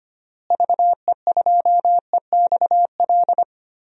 Morse code, heard as a keyed tone, about 25 wpm, 700 Hz. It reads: VE2EXL
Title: DXCC-100 Call Signs - 6 Characters - Repeated 3x - 25wpm